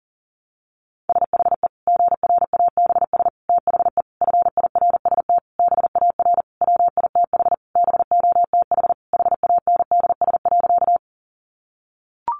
Morse code, heard as a keyed tone, about 40 wpm, 700 Hz. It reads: SHE GRABS THE FIRST BAR WITH BOTH HANDS.
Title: Easy Phrases 40wpm